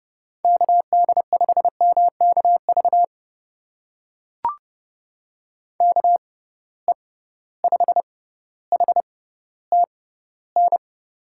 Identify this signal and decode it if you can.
Morse code, heard as a keyed tone, about 31 wpm, spaced as for 15 wpm, 700 Hz. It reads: KD5MKV KE5HTN
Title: USA Call Signs - 6 Characters - 6x Character Spacing - 30wpm